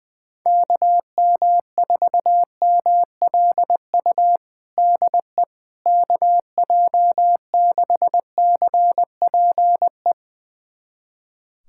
Morse code, heard as a keyed tone, about 20 wpm, 700 Hz. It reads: KM4MLU DE KJ6CPE
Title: USA Callsign DE Callsign 20wpm